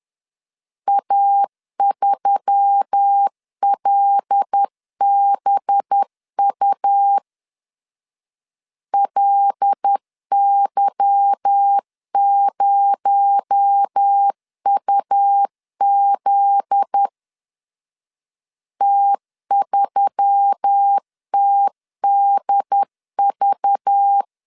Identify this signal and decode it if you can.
Morse code, heard as a keyed tone, about 10 wpm, 800 Hz. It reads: A3LBU LY0UZ T3TDV
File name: LEZIONI / 8 YJ45 / 8 Y 10wpm 2615